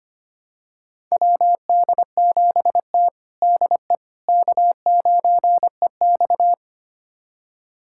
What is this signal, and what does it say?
Morse code, heard as a keyed tone, about 25 wpm, 700 Hz. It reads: WD7T DE K9EX
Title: USA Callsign DE Callsign - 4 Characters - 25wpm